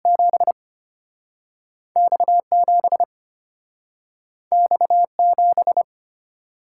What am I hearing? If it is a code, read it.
Morse code, heard as a keyed tone, about 31 wpm, spaced as for 6 wpm, 700 Hz. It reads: 7X 7X 7